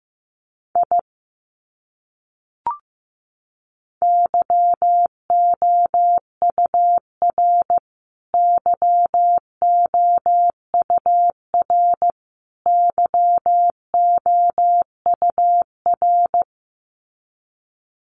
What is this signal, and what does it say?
Morse code, heard as a keyed tone, about 15 wpm, 700 Hz. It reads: I YOUR YOUR YOUR
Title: Top 200 Words Repeated 15wpm